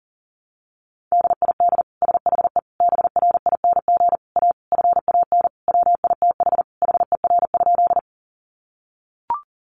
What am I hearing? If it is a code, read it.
Morse code, heard as a keyed tone, about 40 wpm, 700 Hz. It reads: DID SHE BRING A FAN WITH HER?
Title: Sentences from Top 1300 Words 40wpm